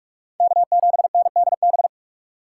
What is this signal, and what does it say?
Morse code, heard as a keyed tone, about 46 wpm, 700 Hz. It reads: K7NDB
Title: USA Call Signs - 5 Characters - 3x Character Spacing - 45wpm